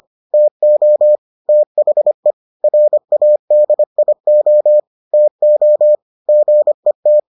Morse code, heard as a keyed tone, about 25 wpm, 600 Hz. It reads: TO THE RADIO TO GET